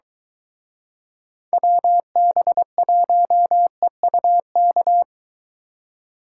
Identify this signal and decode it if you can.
Morse code, heard as a keyed tone, about 22 wpm, 700 Hz. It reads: WB1EUK